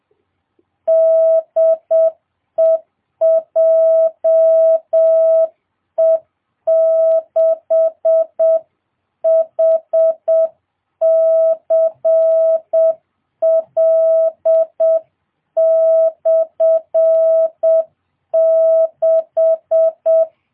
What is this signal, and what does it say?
Morse code, heard as a keyed tone, about 7 wpm, 660 Hz. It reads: DE JE6HCL/6